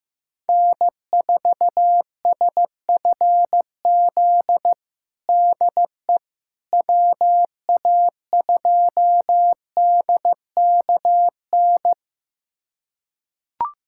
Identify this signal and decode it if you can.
Morse code, heard as a keyed tone, about 15 wpm, 700 Hz. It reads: N4SFZ DE WA2DKN
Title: USA Callsign DE Callsign 15wpm